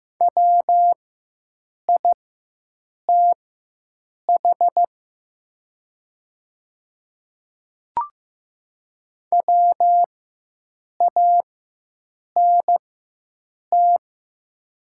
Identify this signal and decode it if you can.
Morse code, heard as a keyed tone, about 15 wpm, spaced as for 7 wpm, 700 Hz. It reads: WITH WANT
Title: 4x Character Spacing - Top 200 Words - 15wpm